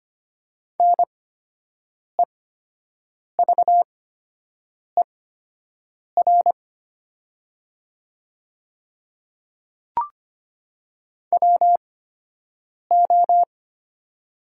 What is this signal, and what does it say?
Morse code, heard as a keyed tone, about 25 wpm, spaced as for 7 wpm, 700 Hz. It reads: NEVER WO
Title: 8x Character Spacing - Top 300 Words - 25wpm